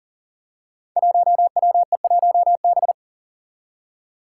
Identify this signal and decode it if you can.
Morse code, heard as a keyed tone, about 40 wpm, 700 Hz. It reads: 1WE1B
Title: Five Challenging Letter-Number Combinations 40wpm